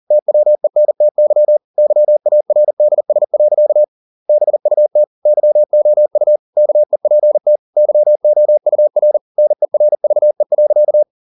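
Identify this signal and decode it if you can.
Morse code, heard as a keyed tone, about 40 wpm, 600 Hz. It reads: TWENTY YARDS. BUT YOU KEPT YOUR NERVE.
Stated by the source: Morse code at 40 wpm, practice text, CW med Gnister.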